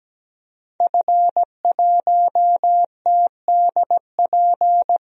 Morse code, heard as a keyed tone, about 17 wpm, 700 Hz. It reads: F1TDP
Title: DXCC-100 Call Signs - 5 Characters - Repeated 2x - 17wpm